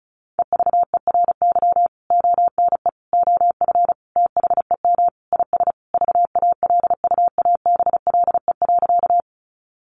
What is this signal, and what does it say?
Morse code, heard as a keyed tone, about 34 wpm, 700 Hz. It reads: EVERY ONE OF THEM IS VALUABLE.
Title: Sentences from Top 1100 Words 35wpm